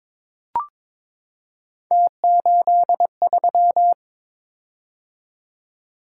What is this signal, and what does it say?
Morse code, heard as a keyed tone, about 21 wpm, 700 Hz. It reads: T83
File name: practice / 22wpm / Three Letter-Number Combinations - 22wpm